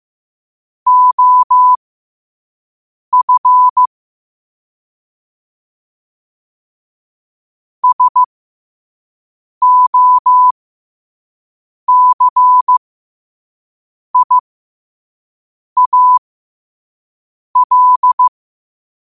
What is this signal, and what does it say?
Morse code, heard as a keyed tone, about 15 wpm, spaced as for 5 wpm, 1000 Hz. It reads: OF SOCIAL
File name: qotdfiles / 5wpm